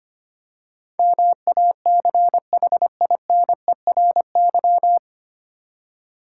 Morse code, heard as a keyed tone, about 25 wpm, 700 Hz. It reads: MACHINERY
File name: practice / 25wpm / 9 Letter Words - First 3 Letters Given - 25wpm